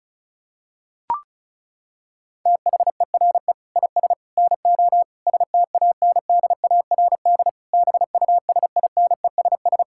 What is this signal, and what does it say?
Morse code, heard as a keyed tone, about 36 wpm, 700 Hz. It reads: THERE IS NO STANDARD BUSINESS
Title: Sentences from Top 1000 Words - Review Part 6 - 35wpm